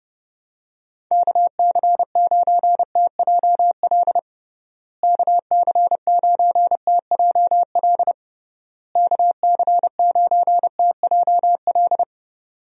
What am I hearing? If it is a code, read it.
Morse code, heard as a keyed tone, about 30 wpm, 700 Hz. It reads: KC9TJL KC9TJL KC9TJL
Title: USA Call Signs - 6 Characters - Repeated 3x - 30wpm